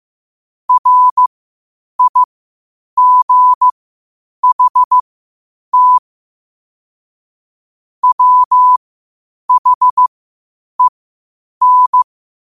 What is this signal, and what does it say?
Morse code, heard as a keyed tone, about 15 wpm, spaced as for 8 wpm, 1000 Hz. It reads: RIGHT WHEN
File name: qotdfiles / 8wpm